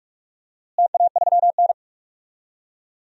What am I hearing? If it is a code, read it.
Morse code, heard as a keyed tone, about 47 wpm, 700 Hz. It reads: TA3N
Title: DXCC-100 Call Signs - 4 Characters - Repeated 2x - 45wpm